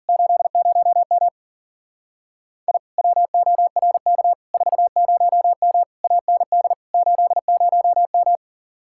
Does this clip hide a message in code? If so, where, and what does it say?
Yes: Morse code from 0.1 to 8.4 s, heard as a keyed tone, about 50 wpm, 700 Hz. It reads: 80M I WORK 40M AND 80M